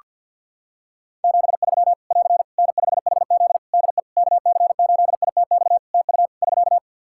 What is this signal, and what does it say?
Morse code, heard as a keyed tone, about 51 wpm, 700 Hz. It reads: 73 <AR> N5HZ DE KC8ITX TU <SK>
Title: Practice QSO - Exchange 4 - Ending - 50wpm